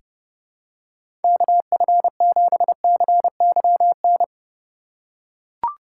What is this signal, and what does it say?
Morse code, heard as a keyed tone, about 32 wpm, 700 Hz. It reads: KF7CYN
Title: USA Call Signs - 6 Characters - 4x Character Spacing - 30wpm